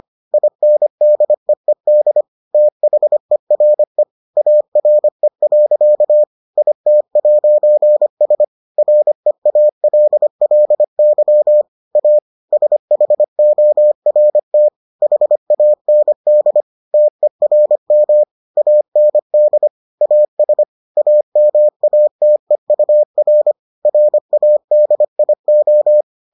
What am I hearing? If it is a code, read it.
Morse code, heard as a keyed tone, about 25 wpm, 600 Hz. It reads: INDEED THERE ARE. IT'S REALLY A SHORT HAND TERM AND AS AMATEUR RADIO